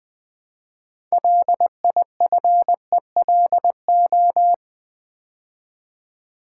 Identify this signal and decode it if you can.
Morse code, heard as a keyed tone, about 20 wpm, 700 Hz. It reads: LIFELO
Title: Build A Word - 3 Letters At A Time - 8-Letter Words - 20wpm